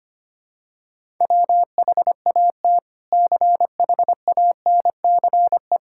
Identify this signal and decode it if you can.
Morse code, heard as a keyed tone, about 25 wpm, 700 Hz. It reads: WHAT CHANCE
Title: Sentences from Top 1000 Words - Review Part 6 - 25wpm